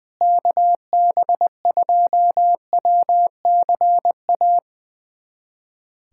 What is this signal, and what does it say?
Morse code, heard as a keyed tone, about 20 wpm, 700 Hz. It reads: KB2WCA
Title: USA Callsign DE Callsign 20wpm